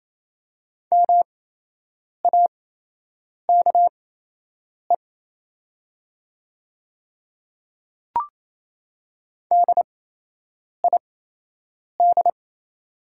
Morse code, heard as a keyed tone, about 28 wpm, spaced as for 8 wpm, 700 Hz. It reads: MAKE DID
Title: 8x Character Spacing - Top 200 Words - 28wpm